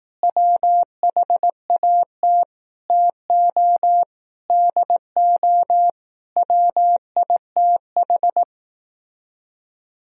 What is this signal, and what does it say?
Morse code, heard as a keyed tone, about 18 wpm, 700 Hz. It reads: WHAT TO DO WITH
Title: Speed-Racing - Sets of 5 Words 15wpm